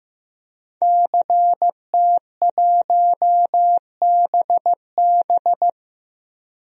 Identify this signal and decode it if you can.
Morse code, heard as a keyed tone, about 15 wpm, 700 Hz. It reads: CT1BB
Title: DXCC-100 Call Signs - 5 Characters - 15wpm